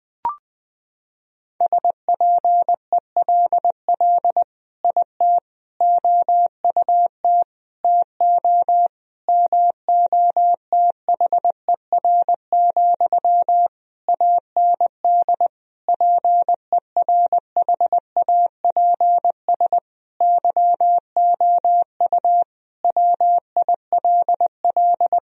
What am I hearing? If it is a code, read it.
Morse code, heard as a keyed tone, about 20 wpm, 700 Hz. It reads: SPELL IT OUT TO MOTHER, AND PERHAPS YOU WILL
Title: Sentences from Top 1000 Words - Review Part 1 - 20wpm